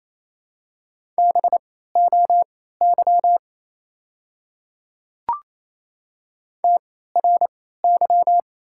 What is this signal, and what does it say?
Morse code, heard as a keyed tone, about 27 wpm, spaced as for 16 wpm, 700 Hz. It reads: BOY TRY
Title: 3x Character Spacing - Top 300 Words - 28wpm